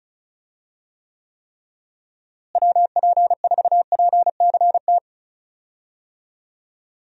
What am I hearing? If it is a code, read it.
Morse code, heard as a keyed tone, about 36 wpm, 700 Hz. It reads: WP4PCT
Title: USA Call Signs - 6 Characters - Repeated 3x - 35wpm